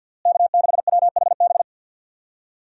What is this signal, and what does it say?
Morse code, heard as a keyed tone, about 51 wpm, 700 Hz. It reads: K6WHB
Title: USA Call Signs - 5 Characters - Repeated 2x - 50wpm